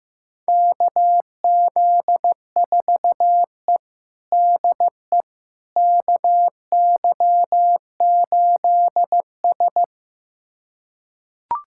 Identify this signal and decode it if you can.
Morse code, heard as a keyed tone, about 15 wpm, 700 Hz. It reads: KZ4E DE KY8S
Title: USA Callsign DE Callsign - 4 Characters - 15wpm